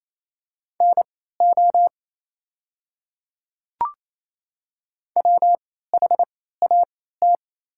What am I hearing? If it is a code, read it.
Morse code, heard as a keyed tone, about 27 wpm, spaced as for 16 wpm, 690 Hz. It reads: NO WHAT
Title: 3x Character Spacing - Top 100 Words - 28wpm